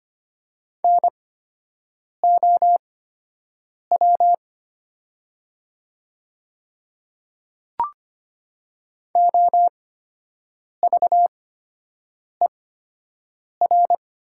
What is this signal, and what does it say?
Morse code, heard as a keyed tone, about 25 wpm, spaced as for 7 wpm, 700 Hz. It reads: NOW OVER
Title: 8x Character Spacing - Top 100 Words - 25wpm